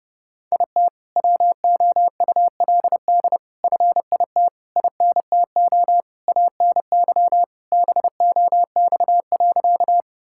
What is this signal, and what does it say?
Morse code, heard as a keyed tone, about 32 wpm, 700 Hz. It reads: IT WOULD FIT INTO ANY BOX.
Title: Sentences from Top 1000 Words - Review Part 5 - 30wpm